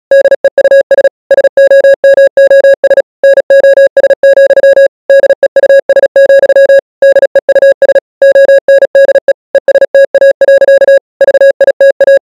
Morse code, heard as a keyed tone, about 37 wpm, 550 Hz. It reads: DEUS SOMOS NOS, DEUS, DEUS ONDE ESTA. VITA